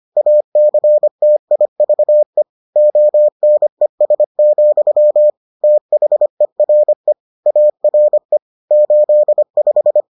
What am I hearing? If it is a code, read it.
Morse code, heard as a keyed tone, about 25 wpm, 600 Hz. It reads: ACTIVE ONES, THERE ARE 85